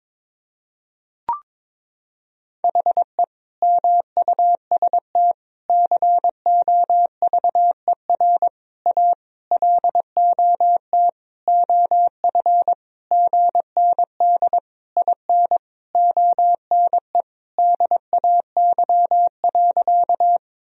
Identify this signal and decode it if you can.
Morse code, heard as a keyed tone, about 22 wpm, 700 Hz. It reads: HE MUST COVER A LOT OF GND IN ONE DAY.
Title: Sentences from Top 500 Words and Abbreviations 22wpm